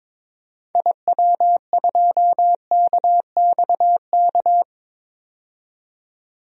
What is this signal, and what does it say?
Morse code, heard as a keyed tone, about 22 wpm, 700 Hz. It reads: IW2KXK
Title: DXCC-100 Call Signs - 6 Characters - 22wpm